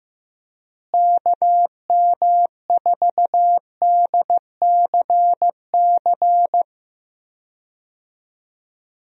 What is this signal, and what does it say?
Morse code, heard as a keyed tone, about 15 wpm, 700 Hz. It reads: KM4DCC
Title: USA Call Signs - 6 Characters - Repeated 3x - 15wpm